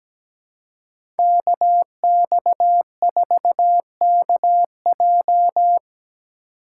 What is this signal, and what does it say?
Morse code, heard as a keyed tone, about 17 wpm, 700 Hz. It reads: KX4KJ
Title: USA Call Signs - 2x Character Spacing - 17wpm